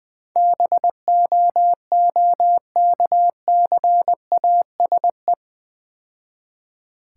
Morse code, heard as a keyed tone, about 22 wpm, 700 Hz. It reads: BOOKCASE
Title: Easy Words - Speed-Racing with Hints - 20wpm